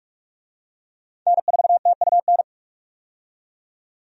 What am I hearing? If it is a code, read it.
Morse code, heard as a keyed tone, about 41 wpm, 700 Hz. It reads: N4TUN
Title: USA Call Signs - 5 Characters - 45wpm